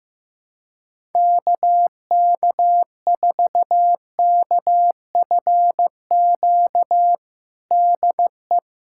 Morse code, heard as a keyed tone, about 15 wpm, 700 Hz. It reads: KK4KFQ DE
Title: USA Callsign DE Callsign - 6 Characters - 15wpm